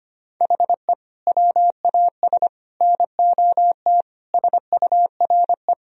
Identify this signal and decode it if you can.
Morse code, heard as a keyed tone, about 25 wpm, 700 Hz. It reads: HE WAS NOT SURE
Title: Sentences from Top 1000 Words - Review Part 1 - 25wpm